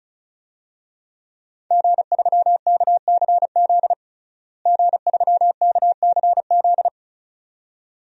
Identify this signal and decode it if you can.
Morse code, heard as a keyed tone, about 34 wpm, 700 Hz. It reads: G3KCZ G3KCZ
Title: DXCC-100 Call Signs - 5 Characters - Repeated 2x - 35wpm